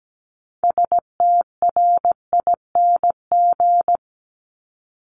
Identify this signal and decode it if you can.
Morse code, heard as a keyed tone, about 17 wpm, 700 Hz. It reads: STRING
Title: Multi-letter Phonemes 17wpm